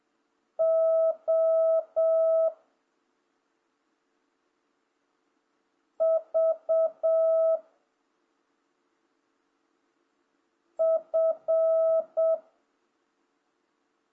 Morse code, heard as a keyed tone, about 7 wpm, 650 Hz. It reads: O V F